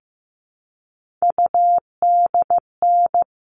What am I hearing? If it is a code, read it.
Morse code, heard as a keyed tone, about 15 wpm, 700 Hz. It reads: UDN
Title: All Cut Numbers - 3-Digit 15wpm